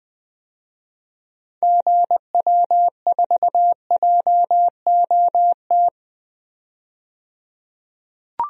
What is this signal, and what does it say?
Morse code, heard as a keyed tone, about 20 wpm, 700 Hz. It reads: GW4JOT
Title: DXCC-100 Call Signs - 6 Characters - 20wpm